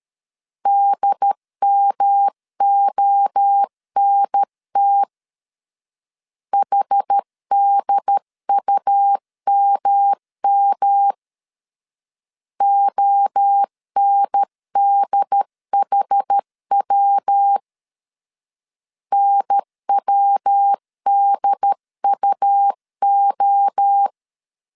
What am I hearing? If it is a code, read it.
Morse code, heard as a keyed tone, about 13 wpm, 800 Hz. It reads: DMONT HDUMM ONDHW NWDUO
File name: CW / DET / 1 3 1500C S A 12wpm 2905